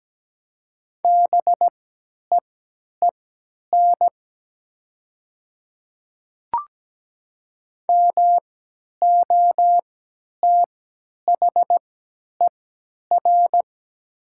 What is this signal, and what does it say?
Morse code, heard as a keyed tone, about 17 wpm, spaced as for 10 wpm, 700 Hz. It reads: BEEN MOTHER
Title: 3x Character Spacing - Top 200 Words - 17wpm